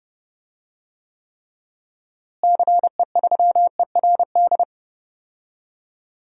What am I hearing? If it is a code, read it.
Morse code, heard as a keyed tone, about 32 wpm, 700 Hz. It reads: CE3ERD
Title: DXCC-100 Call Signs - 6 Characters - Repeated 2x - 30wpm